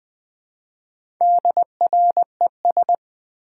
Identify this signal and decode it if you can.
Morse code, heard as a keyed tone, about 20 wpm, 700 Hz. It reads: DRES
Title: Build A Word - 1 Letter At A Time - 5-Letter Words - 20wpm